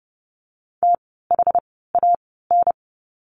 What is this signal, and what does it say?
Morse code, heard as a keyed tone, about 30 wpm, spaced as for 17 wpm, 700 Hz. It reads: THAN
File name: course / 30:10wpm / 108 Lesson 18f - Letter m - Words 30wpm